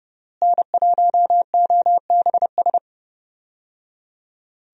Morse code, heard as a keyed tone, about 30 wpm, 700 Hz. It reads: N1OBS
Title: USA Call Signs - 5 Characters - Repeated 3x - 30wpm